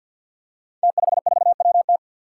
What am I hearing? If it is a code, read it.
Morse code, heard as a keyed tone, about 52 wpm, 700 Hz. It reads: T54WT
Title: Five Challenging Letter-Number Combinations 50wpm